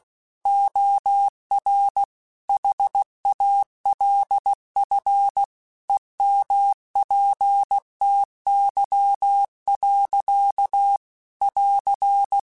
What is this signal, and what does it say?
Morse code, heard as a keyed tone, about 16 wpm, 800 Hz. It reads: OR HALF EMPTY. <AR>
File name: Chuckles / Chuckles-mp3s / Glass Half Full @15